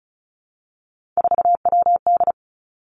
Morse code, heard as a keyed tone, about 36 wpm, 700 Hz. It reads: 4WD
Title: Three Challenging Letter-Number Combinations 35wpm